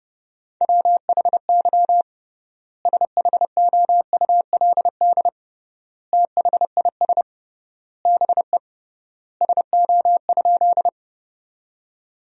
Morse code, heard as a keyed tone, about 32 wpm, 700 Hz. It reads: WHY SHOULD THIS BE SO?
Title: 3x Word-Spacing - Sentences from Top 300 Words - 30wpm